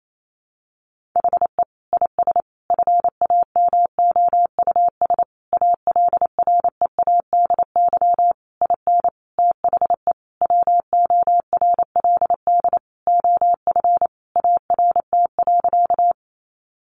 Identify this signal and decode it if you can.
Morse code, heard as a keyed tone, about 28 wpm, 700 Hz. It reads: HE IS FAMOUS ALREADY IN THE WORLD OF ART.
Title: Sentences from Top 1500 Words 28wpm